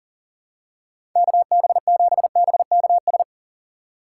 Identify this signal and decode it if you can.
Morse code, heard as a keyed tone, about 40 wpm, 700 Hz. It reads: KB7BKS
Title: USA Call Signs - 6 Characters - 4x Character Spacing - 40wpm